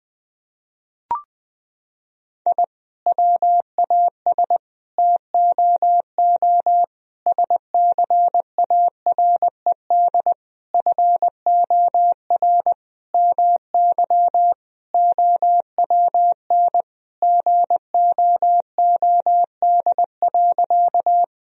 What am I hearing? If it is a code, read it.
Morse code, heard as a keyed tone, about 21 wpm, 700 Hz. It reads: I WAS TOO SCARED FOR MY OWN GOOD.
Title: Sentences from Top 1000 Words - Review Part 4 - 20wpm